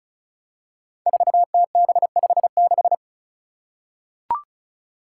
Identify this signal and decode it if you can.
Morse code, heard as a keyed tone, about 34 wpm, 700 Hz. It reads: 4TB56
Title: Five Challenging Letter-Number Combinations 35wpm